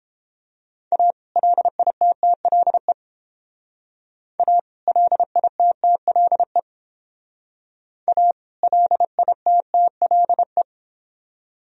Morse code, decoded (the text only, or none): A LITTLE A LITTLE A LITTLE